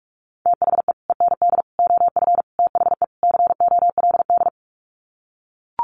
Morse code, heard as a keyed tone, about 44 wpm, 700 Hz. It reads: THE END OF THE COLD
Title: Sets of 5 Words 45wpm